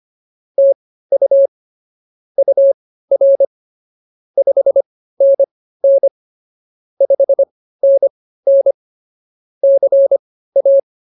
Morse code, heard as a keyed tone, about 24 wpm, spaced as for 15 wpm, 550 Hz. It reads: TU UR 5NN 5NN CA